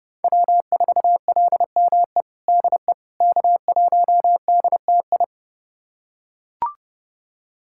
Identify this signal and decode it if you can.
Morse code, heard as a keyed tone, about 30 wpm, 700 Hz. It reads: W4LME DE K1DTI
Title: USA Callsign DE Callsign - 5 Characters - 30wpm